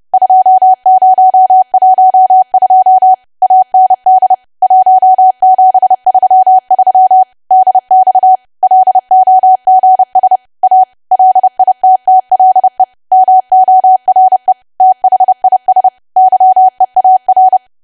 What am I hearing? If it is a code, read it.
Morse code, heard as a keyed tone, about 30 wpm, 750 Hz. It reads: 2012 AND 1733 DX LOGS A LITTLE MORE THIS YEAR
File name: cwa / W1AW-Level3 / 30WPM